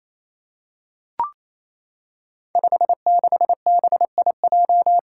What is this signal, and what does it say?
Morse code, heard as a keyed tone, about 28 wpm, 700 Hz. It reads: 56BIJ
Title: Five Challenging Letter-Number Combinations 28wpm